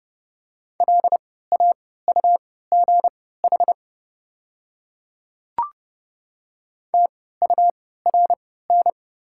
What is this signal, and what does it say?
Morse code, heard as a keyed tone, about 30 wpm, spaced as for 17 wpm, 700 Hz. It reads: LAUGH TURN
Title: 3x Character Spacing - Top 500 Words - 30wpm